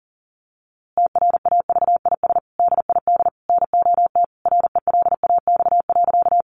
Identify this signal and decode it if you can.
Morse code, heard as a keyed tone, about 40 wpm, 700 Hz. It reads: TRAVIS DID NOT RELAX.
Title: A Sound Of Thunder 40wpm